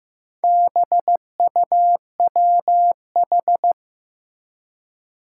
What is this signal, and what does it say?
Morse code, heard as a keyed tone, about 15 wpm, 700 Hz. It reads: BUWH
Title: Four Challenging Letter-Number Combinations 15wpm